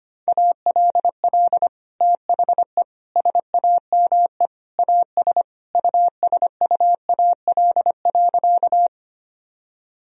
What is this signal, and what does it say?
Morse code, heard as a keyed tone, about 25 wpm, 700 Hz. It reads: ALL THE SAME AS USUAL.
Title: Sentences from Top 1000 Words - Review Part 2 - 25wpm